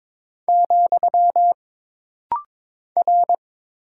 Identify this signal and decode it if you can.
Morse code, heard as a keyed tone, about 22 wpm, 700 Hz. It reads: , R
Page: This rapid-fire timing is geared to push your ICR (Instant Character Recognition) skills.